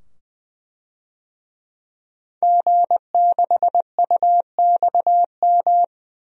Morse code, heard as a keyed tone, about 20 wpm, 700 Hz. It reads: G6UXM
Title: DXCC-100 Call Signs - 5 Characters - 20wpm